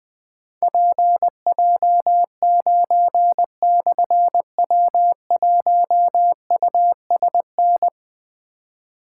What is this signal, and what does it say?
Morse code, heard as a keyed tone, about 20 wpm, 700 Hz. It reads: PJ9/W1USN
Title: Difficult Call Signs 20wpm